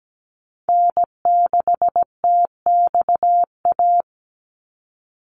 Morse code, heard as a keyed tone, about 17 wpm, 700 Hz. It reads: N6TXA
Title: USA Call Signs 17wpm